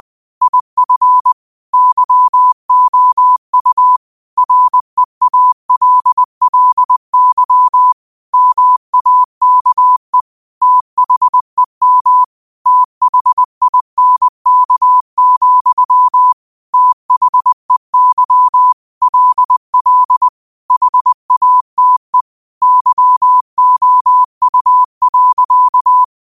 Morse code, heard as a keyed tone, about 20 wpm, 1000 Hz. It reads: IF YOU REALLY MAKE THEM THINK, THEY LL HATE YOU.
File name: qotdfiles / 20wpm